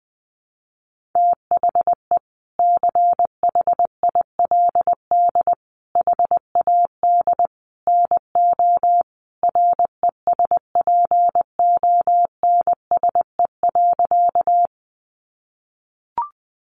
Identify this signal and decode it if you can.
Morse code, heard as a keyed tone, about 20 wpm, 700 Hz. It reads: THE CHILD HAD NO RESPONSE.
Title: Sentences from Top 1400 Words 20wpm